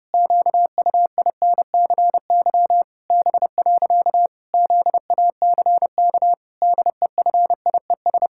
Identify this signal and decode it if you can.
Morse code, heard as a keyed tone, about 30 wpm, 700 Hz. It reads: QUINCY B. ZACK DEFIES
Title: Pangram Challenge v2 - 30wpm